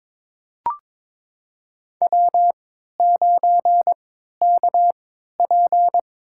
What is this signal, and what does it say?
Morse code, heard as a keyed tone, about 22 wpm, spaced as for 13 wpm, 700 Hz. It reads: W9KP
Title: USA Call Signs - 3x Character Spacing - 22wpm